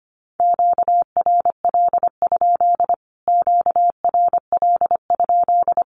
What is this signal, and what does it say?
Morse code, heard as a keyed tone, about 25 wpm, 700 Hz. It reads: QRL? QRL?
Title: Q-Codes Repeated 3x 25wpm